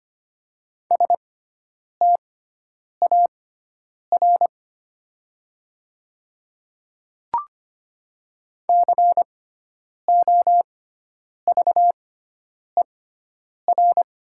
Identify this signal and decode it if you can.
Morse code, heard as a keyed tone, about 25 wpm, spaced as for 9 wpm, 700 Hz. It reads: STAR COVER
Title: 6x Character Spacing - Top 500 Words - 25wpm